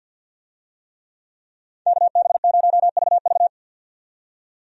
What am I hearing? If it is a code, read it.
Morse code, heard as a keyed tone, about 52 wpm, 700 Hz. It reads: KB0VV
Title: USA Call Signs - 5 Characters - 50wpm